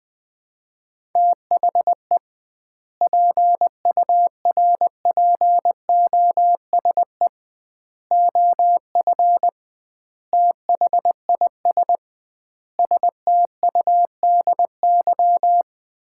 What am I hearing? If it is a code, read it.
Morse code, heard as a keyed tone, about 20 wpm, 700 Hz. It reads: THE PURPOSE OF THIS STUDY
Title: 2x Word-Spacing - Sets of 5 Words - 20wpm